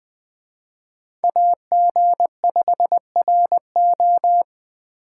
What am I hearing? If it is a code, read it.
Morse code, heard as a keyed tone, about 20 wpm, 700 Hz. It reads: AG5RO